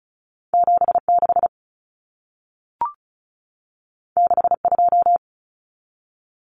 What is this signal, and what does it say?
Morse code, heard as a keyed tone, about 36 wpm, 700 Hz. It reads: 76 62
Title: Common Cut Numbers - 2-Digit 35wpm